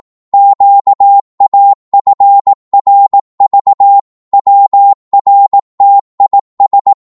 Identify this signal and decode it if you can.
Morse code, heard as a keyed tone, about 18 wpm, 800 Hz. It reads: QAFRV WRTIS
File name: Morse / ltr-18wpm